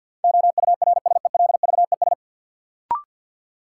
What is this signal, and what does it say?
Morse code, heard as a keyed tone, about 53 wpm, 700 Hz. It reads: OURSELVES
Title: Sentences from Top 1600 Words - New Word Drill Alt - 50wpm